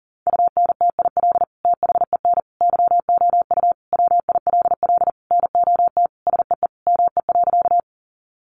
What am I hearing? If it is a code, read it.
Morse code, heard as a keyed tone, about 40 wpm, 700 Hz. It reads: UNTIL THEN YOU WILL NOT SEE ME.
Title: Sentences from Top 600 Words 40wpm